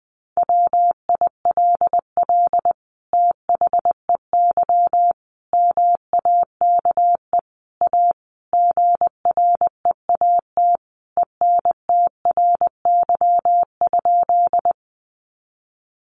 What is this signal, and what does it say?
Morse code, heard as a keyed tone, about 20 wpm, 700 Hz. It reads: WILL THEY MAKE A GREAT ENTRY?
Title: Sentences from Top 1500 Words 20wpm